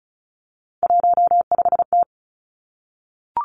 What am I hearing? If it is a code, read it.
Morse code, heard as a keyed tone, about 36 wpm, 700 Hz. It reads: 15T
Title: Three Challenging Letter-Number Combinations 35wpm